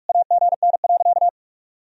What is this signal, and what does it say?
Morse code, heard as a keyed tone, about 41 wpm, 700 Hz. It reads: AGN.
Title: Sentences from Top 500 Words and Abbreviations 45wpm